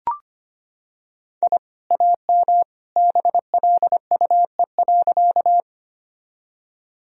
Morse code, heard as a keyed tone, about 25 wpm, 700 Hz. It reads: I AM BLUE.
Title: Sentences from Top 1000 Words - Review Part 3 - 25wpm